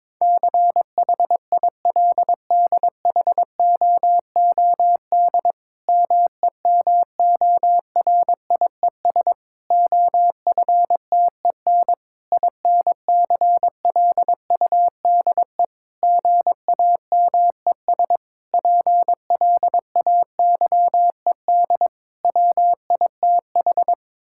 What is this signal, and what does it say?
Morse code, heard as a keyed tone, about 22 wpm, 700 Hz. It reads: CHILDHOOD MEMORIES OFTEN INCLUDE GAMES PLAYED WITH
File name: Difficult Sentences with Hints - 22wpm